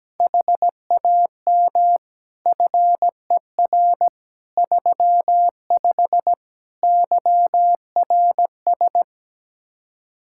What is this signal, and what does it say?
Morse code, heard as a keyed tone, about 17 wpm, 700 Hz. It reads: HAM FER 35 YRS
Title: K1USN SST QSO Practice 17wpm